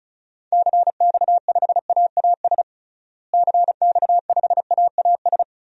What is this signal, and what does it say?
Morse code, heard as a keyed tone, about 34 wpm, 700 Hz. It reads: CX5AAS CX5AAS
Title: DXCC-100 Call Signs - 6 Characters - Repeated 3x - 35wpm